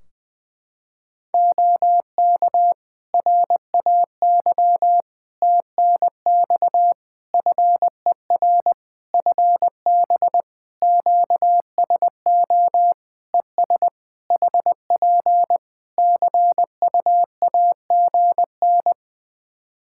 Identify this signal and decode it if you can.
Morse code, heard as a keyed tone, about 20 wpm, 700 Hz. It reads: OK RAY TNX FER FB QSO ES HP CUAGN
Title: Practice QSO - Exchange 4 - Ending - 20wpm